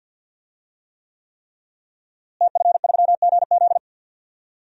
Morse code, heard as a keyed tone, about 51 wpm, 700 Hz. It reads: TF3GZ
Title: DXCC-100 Call Signs - 5 Characters - Repeated 3x - 50wpm